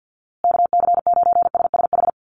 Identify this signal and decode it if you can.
Morse code, heard as a keyed tone, about 48 wpm, 700 Hz. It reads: KC9SSH
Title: USA Call Signs Repeated 2x 50wpm